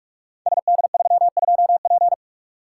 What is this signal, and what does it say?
Morse code, heard as a keyed tone, about 44 wpm, 690 Hz. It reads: SD32P
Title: Code Groups - 5 Letter:Numbers - 45wpm